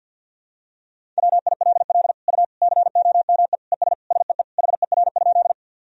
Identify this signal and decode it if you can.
Morse code, heard as a keyed tone, about 49 wpm, 700 Hz. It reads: WILL U COME ES SEE HER?
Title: Sentences from Top 100 Words and Abbreviations 50wpm